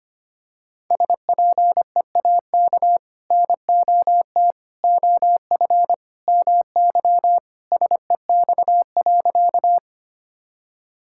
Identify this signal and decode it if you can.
Morse code, heard as a keyed tone, about 25 wpm, 700 Hz. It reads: SPEAK NOT OF MY SEX.
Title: Sentences from Top 1600 Words 25wpm